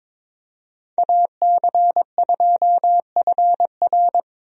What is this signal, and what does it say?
Morse code, heard as a keyed tone, about 22 wpm, 700 Hz. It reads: AC2FR
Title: USA Call Signs - 5 Characters - Repeated 3x - 22wpm